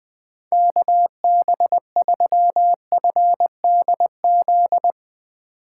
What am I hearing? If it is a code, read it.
Morse code, heard as a keyed tone, about 21 wpm, 700 Hz. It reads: KB3FDZ
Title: USA Call Signs - 6 Characters - 2x Character Spacing - 20wpm